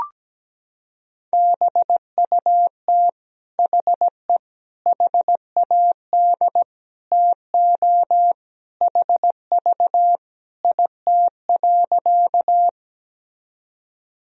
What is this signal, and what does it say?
Morse code, heard as a keyed tone, about 17 wpm, 700 Hz. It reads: BUT HE HAD TO HV IT.
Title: Sentences from Top 100 Words and Abbreviations 17wpm